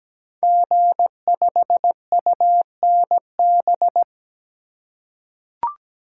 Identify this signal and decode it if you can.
Morse code, heard as a keyed tone, about 17 wpm, 700 Hz. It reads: G5UNB
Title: Code Groups - 5 Letter:Numbers - 17wpm